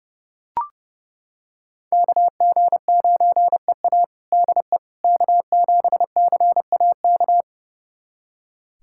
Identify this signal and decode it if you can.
Morse code, heard as a keyed tone, about 30 wpm, 700 Hz. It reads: KG9EA DE K7CAK
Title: USA Callsign DE Callsign - 5 Characters - 30wpm